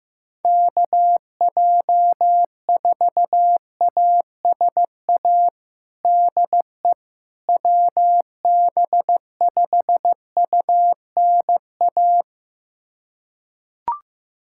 Morse code, heard as a keyed tone, about 15 wpm, 700 Hz. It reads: KJ4ASA DE WB5UNA
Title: USA Callsign DE Callsign - 6 Characters - 15wpm